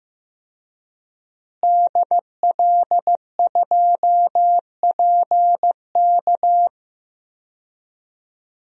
Morse code, heard as a keyed tone, about 15 wpm, 700 Hz. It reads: DL2PK
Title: DXCC-100 Call Signs - 5 Characters - Repeated 3x - 15wpm